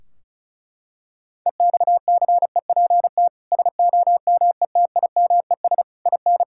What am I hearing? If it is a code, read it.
Morse code, heard as a keyed tone, about 34 wpm, 700 Hz. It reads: EXCEPT SOMETIMES IN
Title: Sentences from Top 1600 Words 35wpm